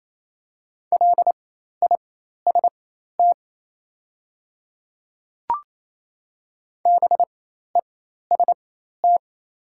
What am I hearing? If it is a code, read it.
Morse code, heard as a keyed tone, about 28 wpm, spaced as for 13 wpm, 690 Hz. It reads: LIST BEST
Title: 4x Character Spacing - Top 500 Words - 28wpm